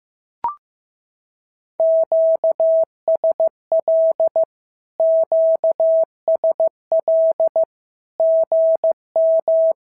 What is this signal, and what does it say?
Morse code, heard as a keyed tone, about 15 wpm, 650 Hz. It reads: QSL QSL GM
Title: SOTA QSO Practice 15wpm